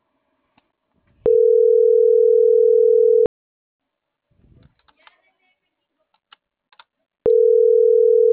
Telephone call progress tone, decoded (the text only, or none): ringback tone